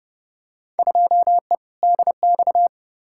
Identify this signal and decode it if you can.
Morse code, heard as a keyed tone, about 30 wpm, 700 Hz. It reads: 2E DX
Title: Field Day Exchanges - 30wpm